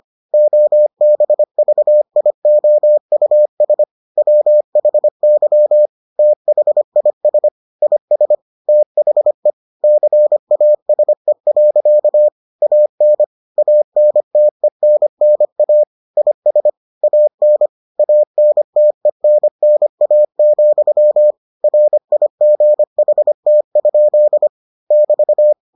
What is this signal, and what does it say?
Morse code, heard as a keyed tone, about 25 wpm, 600 Hz. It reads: OBVIOUS WHY THIS IS THE CASE. AN ANTENNA IS AN ANTENNA, RIGHT? <BT>